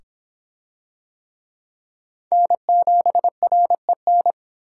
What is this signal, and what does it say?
Morse code, heard as a keyed tone, about 25 wpm, 700 Hz. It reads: N7REN